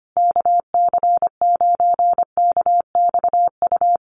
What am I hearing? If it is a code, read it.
Morse code, heard as a keyed tone, about 25 wpm, 700 Hz. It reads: KC9KXU
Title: USA Call Signs Repeated 3x 25wpm